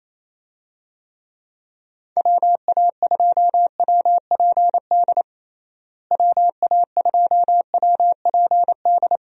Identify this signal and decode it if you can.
Morse code, heard as a keyed tone, about 28 wpm, 700 Hz. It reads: WA2WPD WA2WPD
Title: USA Call Signs - 6 Characters - Repeated 2x - 28wpm